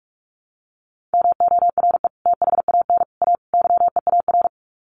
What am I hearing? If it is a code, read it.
Morse code, heard as a keyed tone, about 44 wpm, 700 Hz. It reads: MORE THAN A YEAR
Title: Sets of 4 Words 45wpm